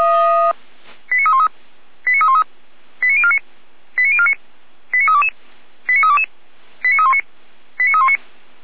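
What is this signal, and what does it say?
Five-tone selective call (ZVEI), heard as quick runs of five tones, tones of about 70 ms. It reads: 89313 89313 89049 89049 89322 89322 89319 89319
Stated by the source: ZVEI five-tone call received on SDR, fire brigade Feuerwehr Orlamünde.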